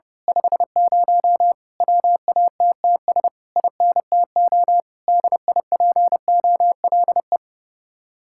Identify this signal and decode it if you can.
Morse code, heard as a keyed tone, about 30 wpm, 700 Hz. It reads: 50 WATTS INTO DIPOLE
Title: Speed-Racing Common QSO Phrases - 30wpm